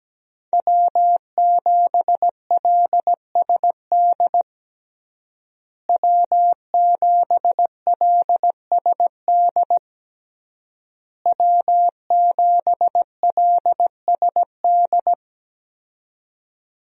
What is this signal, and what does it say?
Morse code, heard as a keyed tone, about 17 wpm, 700 Hz. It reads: W7LSD W7LSD W7LSD
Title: USA Call Signs - 5 Characters - Repeated 3x - 17wpm